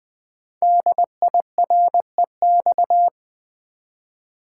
Morse code, heard as a keyed tone, about 20 wpm, 700 Hz. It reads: DIREX
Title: Code Groups - 5 Letters - 20wpm